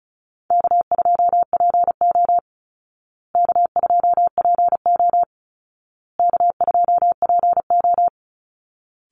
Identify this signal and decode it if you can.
Morse code, heard as a keyed tone, about 34 wpm, 700 Hz. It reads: K2PO K2PO K2PO
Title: Call Signs Repeated 2x - 35wpm